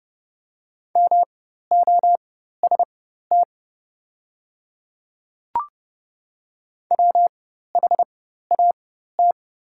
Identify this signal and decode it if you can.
Morse code, heard as a keyed tone, about 30 wpm, spaced as for 14 wpm, 700 Hz. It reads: MOST WHAT
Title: 4x Character Spacing - Top 200 Words - 30wpm